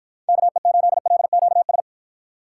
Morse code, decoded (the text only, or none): KE8LQS